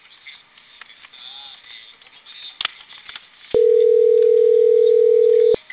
Ringback tone (North America), 440 + 480 Hz.